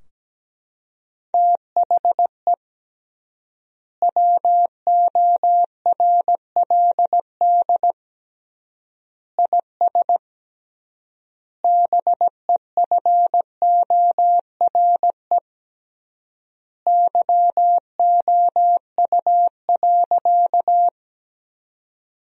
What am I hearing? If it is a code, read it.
Morse code, heard as a keyed tone, about 17 wpm, 700 Hz. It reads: THE WORLD IS BEFORE YOU.
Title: 3x Word-Spacing - Sentences from Top 200 Words - 17wpm